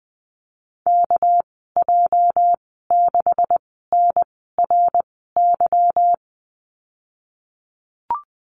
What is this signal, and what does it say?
Morse code, heard as a keyed tone, about 20 wpm, spaced as for 15 wpm, 700 Hz. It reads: KJ6NRY
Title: 207 Lesson 33g - Letter j - Call Signs 20wpm